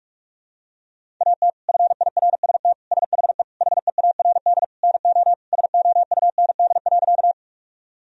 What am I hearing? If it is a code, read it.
Morse code, heard as a keyed tone, about 42 wpm, 700 Hz. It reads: AT FIRST SHE HEARD NO SOUND.
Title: Sentences from Top 1000 Words - Review Part 1 - 45wpm